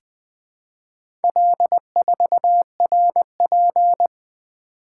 Morse code, heard as a keyed tone, about 20 wpm, 700 Hz. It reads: L4RP
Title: Four Letter-Number Combinations - 20wpm